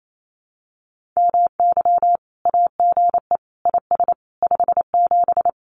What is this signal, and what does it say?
Morse code, heard as a keyed tone, about 28 wpm, 700 Hz. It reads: MY AGE IS 57
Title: Common QSO Phrases Encore 28wpm